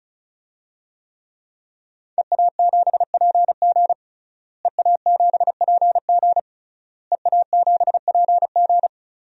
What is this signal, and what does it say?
Morse code, heard as a keyed tone, about 36 wpm, 700 Hz. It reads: EA7PG EA7PG EA7PG
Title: DXCC-100 Call Signs - 5 Characters - Repeated 3x - 35wpm